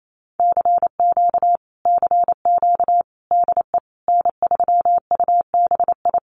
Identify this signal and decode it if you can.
Morse code, heard as a keyed tone, about 28 wpm, 700 Hz. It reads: CQ CQ DE N3UBI
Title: Common QSO Phrases Encore 28wpm